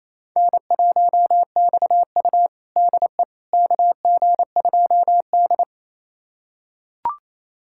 Morse code, heard as a keyed tone, about 28 wpm, 700 Hz. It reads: N1XU DE KG2D
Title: USA Callsign DE Callsign - 4 Characters - 28wpm